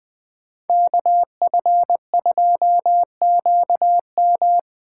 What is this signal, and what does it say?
Morse code, heard as a keyed tone, about 20 wpm, 700 Hz. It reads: KF2QM
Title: USA Call Signs - 5 Characters - 8x Character Spacing - 20wpm